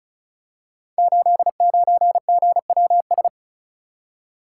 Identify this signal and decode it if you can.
Morse code, heard as a keyed tone, about 36 wpm, 700 Hz. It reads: 89GWS
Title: Code Groups - 5 Letter:Numbers - 35wpm